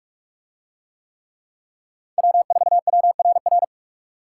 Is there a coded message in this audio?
Yes: Morse code from 2.2 to 3.6 s, heard as a keyed tone, about 42 wpm, 700 Hz. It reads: W4WRR